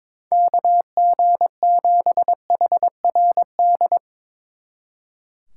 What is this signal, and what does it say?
Morse code, heard as a keyed tone, about 24 wpm, 700 Hz. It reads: KG7HRD
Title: Speed Racing - USA Call Signs 22wpm